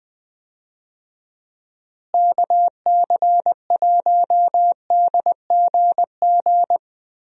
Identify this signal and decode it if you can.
Morse code, heard as a keyed tone, about 20 wpm, 700 Hz. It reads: KC1DGG
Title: USA Call Signs - 6 Characters - Repeated 2x - 20wpm